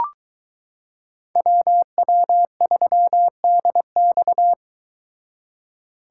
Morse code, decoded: WW3DX

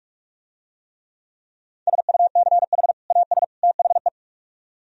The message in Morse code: SUCH AS THE

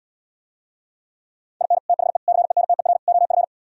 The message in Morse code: IN THE CENTER OF

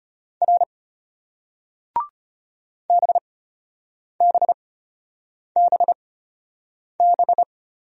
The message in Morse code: R BBBB

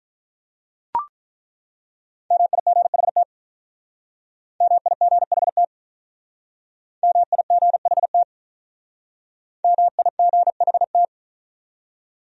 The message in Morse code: MIGHT MIGHT MIGHT MIGHT

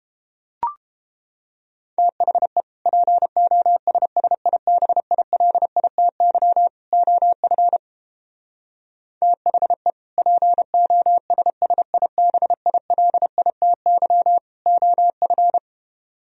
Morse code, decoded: THE POSSIBILITY OF THE POSSIBILITY OF

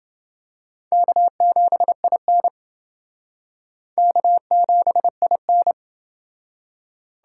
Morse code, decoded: K7IN K7IN

